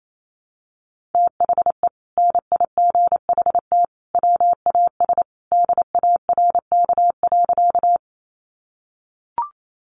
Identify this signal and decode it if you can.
Morse code, heard as a keyed tone, about 28 wpm, 700 Hz. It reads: THE NIGHT WAS DARK.
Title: Sentences from Top 500 Words 28wpm